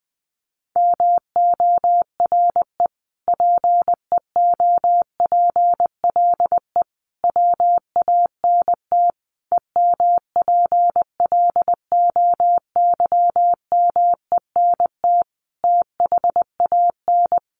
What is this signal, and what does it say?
Morse code, heard as a keyed tone, about 19 wpm, 700 Hz. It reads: MORE PEOPLE WANT EMPLOYMENT THAN
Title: Sentences from Top 1500 Words 20wpm